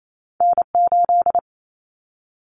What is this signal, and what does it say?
Morse code, heard as a keyed tone, about 28 wpm, 700 Hz. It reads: N8
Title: Callsign Prefixes for Top 25 DXCC - Repeated 3x - 28wpm